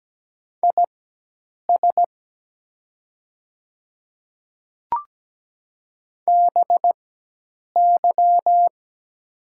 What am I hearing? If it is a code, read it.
Morse code, heard as a keyed tone, about 17 wpm, spaced as for 8 wpm, 700 Hz. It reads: IS BY
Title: 4x Character Spacing - Top 100 Words - 17wpm